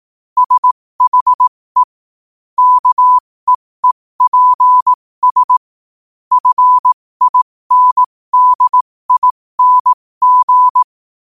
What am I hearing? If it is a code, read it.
Morse code, heard as a keyed tone, about 18 wpm, spaced as for 15 wpm, 1000 Hz. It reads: SHE KEEPS FINDING
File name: qotdfiles / 15wpm